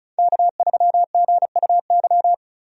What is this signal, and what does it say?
Morse code, heard as a keyed tone, about 36 wpm, 700 Hz. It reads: K3GUY
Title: USA Call Signs - 5 Characters - 35wpm